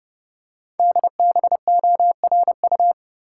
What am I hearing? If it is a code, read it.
Morse code, heard as a keyed tone, about 30 wpm, 700 Hz. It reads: DBORU
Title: Code Groups - 5 Letters - 30wpm